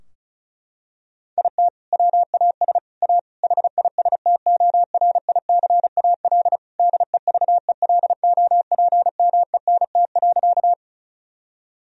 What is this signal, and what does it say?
Morse code, heard as a keyed tone, about 36 wpm, 700 Hz. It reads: IT WAS A HISTORICAL DEVELOPMENT.
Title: Sentences from Top 1000 Words - Review Part 4 - 35wpm